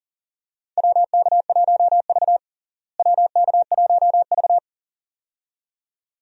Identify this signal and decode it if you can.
Morse code, heard as a keyed tone, about 40 wpm, 700 Hz. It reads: WK1V WK1V
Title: USA Call Signs - 4 Characters - Repeated 2x - 40wpm